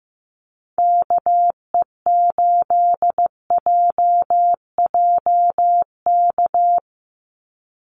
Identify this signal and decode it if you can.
Morse code, heard as a keyed tone, about 15 wpm, 700 Hz. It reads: KE8JJK
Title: USA Call Signs 15wpm